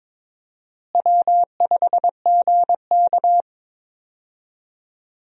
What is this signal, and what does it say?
Morse code, heard as a keyed tone, about 22 wpm, 700 Hz. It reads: W5GK
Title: USA Call Signs - 4 Characters - Repeated 3x - 22wpm